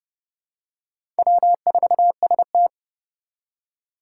Morse code, heard as a keyed tone, about 30 wpm, 700 Hz. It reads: W4ST
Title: Four Challenging Letter-Number Combinations 30wpm